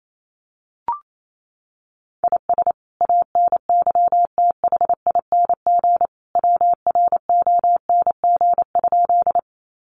Morse code, heard as a keyed tone, about 28 wpm, 700 Hz. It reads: IS ANYTHING WRONG?
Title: Sentences from Top 1000 Words 28wpm